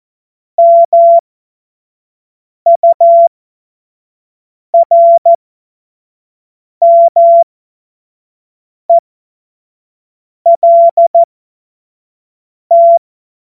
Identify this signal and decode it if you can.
Morse code, heard as a keyed tone, about 14 wpm, spaced as for 5 wpm, 680 Hz. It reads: MURMELT
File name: Morsekurs / Witze / Witz 19 Dezember 2023 25 F70